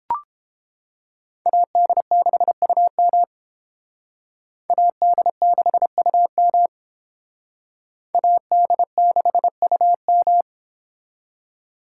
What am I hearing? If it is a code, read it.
Morse code, heard as a keyed tone, about 30 wpm, 700 Hz. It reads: AD6UM AD6UM AD6UM